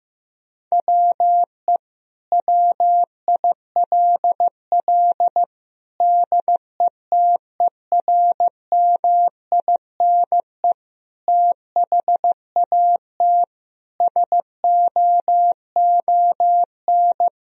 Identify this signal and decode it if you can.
Morse code, heard as a keyed tone, about 15 wpm, 700 Hz. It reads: WE WILL DETERMINE THAT SOON
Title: Sentences from Top 1000 Words - Review Part 4 - 15wpm